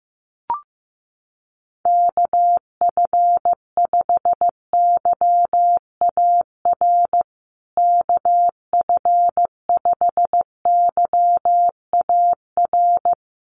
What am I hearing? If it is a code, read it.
Morse code, heard as a keyed tone, about 15 wpm, 700 Hz. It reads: KF5YAR KF5YAR
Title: USA Call Signs Repeated 3x 15wpm